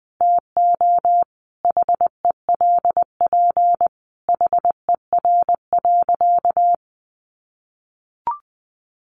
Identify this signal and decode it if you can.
Morse code, heard as a keyed tone, about 20 wpm, 700 Hz. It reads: TO HELP HER.
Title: Sentences from Top 1000 Words 20wpm